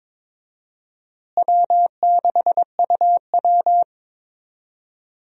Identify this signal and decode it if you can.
Morse code, heard as a keyed tone, about 22 wpm, 700 Hz. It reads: W6UW
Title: USA Call Signs - 4 Characters - Repeated 3x - 22wpm